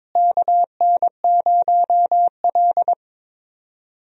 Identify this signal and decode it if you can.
Morse code, heard as a keyed tone, about 22 wpm, 700 Hz. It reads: KN0L
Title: USA Call Signs - 4 Characters - Repeated 2x - 22wpm